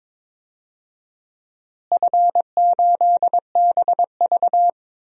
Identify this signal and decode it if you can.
Morse code, heard as a keyed tone, about 23 wpm, 700 Hz. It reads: F8BV
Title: DXCC-100 Call Signs - 4 Characters - Repeated 3x - 22wpm